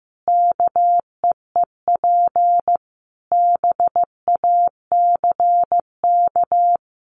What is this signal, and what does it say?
Morse code, heard as a keyed tone, about 15 wpm, 700 Hz. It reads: KEEP BACK
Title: Sentences from Top 1300 Words 15wpm